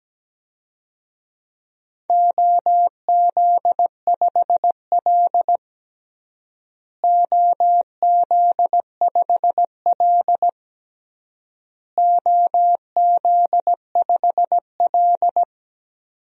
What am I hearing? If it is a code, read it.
Morse code, heard as a keyed tone, about 17 wpm, 700 Hz. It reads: OZ5L OZ5L OZ5L
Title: DXCC-100 Call Signs - 4 Characters - Repeated 3x - 17wpm